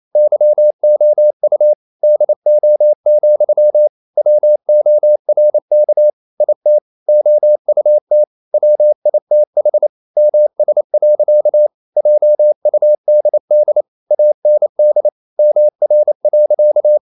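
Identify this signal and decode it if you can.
Morse code, heard as a keyed tone, about 28 wpm, 600 Hz. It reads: YOU DO, WORK IT OUT WITH MS. JUDD AND MR.
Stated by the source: Morse code practice at 28 wpm, text of Ken Follett's novel Never in English, from CW med Gnister.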